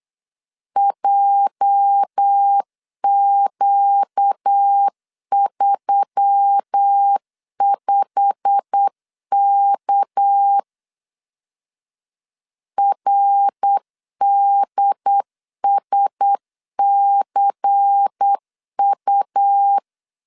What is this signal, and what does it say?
Morse code, heard as a keyed tone, about 8 wpm, 800 Hz. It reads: JQ35K RDSCU
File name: LEZIONI / 8 YJ45 / 8 YJ45 8wpm 3214